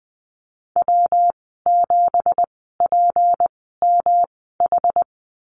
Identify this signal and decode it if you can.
Morse code, heard as a keyed tone, about 19 wpm, spaced as for 14 wpm, 700 Hz. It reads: W7PMH
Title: 207 Lesson 33g - Letter j - Call Signs 20wpm